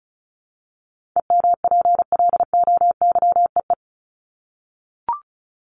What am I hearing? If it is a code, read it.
Morse code, heard as a keyed tone, about 34 wpm, 700 Hz. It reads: EMPLOYEE
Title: Top 1501-1750 Words 35wpm